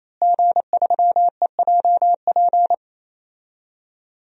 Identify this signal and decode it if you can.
Morse code, heard as a keyed tone, about 28 wpm, 700 Hz. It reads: G3EJP
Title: DXCC-100 Call Signs - 5 Characters - Repeated 3x - 28wpm